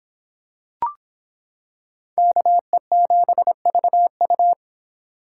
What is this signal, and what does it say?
Morse code, heard as a keyed tone, about 26 wpm, 700 Hz. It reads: KE7VU